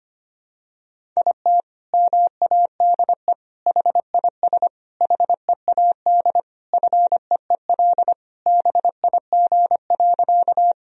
Morse code, heard as a keyed tone, about 25 wpm, 700 Hz. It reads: IT MADE HIS HEAD FEEL BIG.
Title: Sentences from Top 1000 Words - Review Part 1 - 25wpm